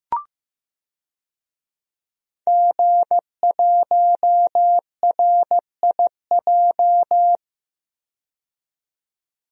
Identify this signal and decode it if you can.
Morse code, heard as a keyed tone, about 15 wpm, 700 Hz. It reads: G1RIJ
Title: DXCC-100 Call Signs - 5 Characters - 15wpm